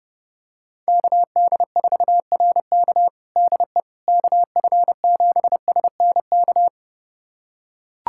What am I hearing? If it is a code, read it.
Morse code, heard as a keyed tone, about 30 wpm, 700 Hz. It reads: KD4RK DE KF7SNK
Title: USA Callsign DE Callsign 30wpm